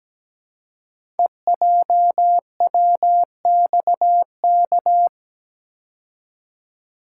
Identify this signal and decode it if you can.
Morse code, heard as a keyed tone, about 17 wpm, 700 Hz. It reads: EJWXK
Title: Code Groups - 5 Letters - 17wpm